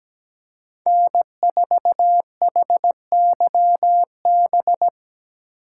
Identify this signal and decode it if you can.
Morse code, heard as a keyed tone, about 20 wpm, 700 Hz. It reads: N4HYB
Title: Speed Racing - USA 1x3 Call Signs 17wpm